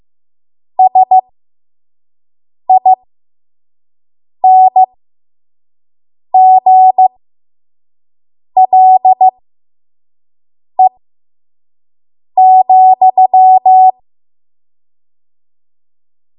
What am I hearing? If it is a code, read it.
Morse code, heard as a keyed tone, about 15 wpm, spaced as for 5 wpm, 750 Hz. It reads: SINGLE,
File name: Morse / 05WPM